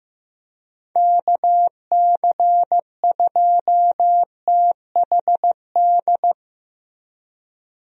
Morse code, heard as a keyed tone, about 15 wpm, 700 Hz. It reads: KC2THD
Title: USA Call Signs - 6 Characters - Repeated 3x - 15wpm